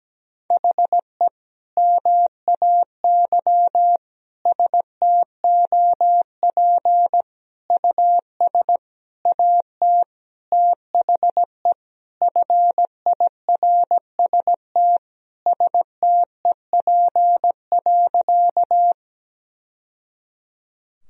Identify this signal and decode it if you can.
Morse code, heard as a keyed tone, about 17 wpm, 700 Hz. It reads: HE MAY STOP US AT THE FIRST STEP.
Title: Sentences from Top 1000 Words - Review Part 2 - 17wpm